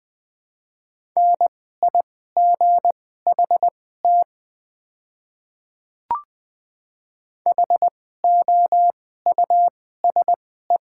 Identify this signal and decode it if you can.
Morse code, heard as a keyed tone, about 20 wpm, spaced as for 15 wpm, 700 Hz. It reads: NIGHT HOUSE
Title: 2x Character Spacing - Top 300 Words - 20wpm